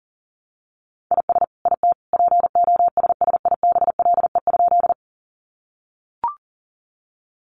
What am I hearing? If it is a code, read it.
Morse code, heard as a keyed tone, about 40 wpm, 700 Hz. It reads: IS IT POSSIBLE?
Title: Sentences from Top 500 Words 40wpm